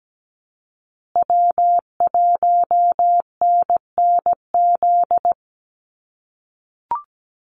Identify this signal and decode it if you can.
Morse code, heard as a keyed tone, about 17 wpm, 700 Hz. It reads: W1NNZ
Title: USA Call Signs Repeated 3x 17wpm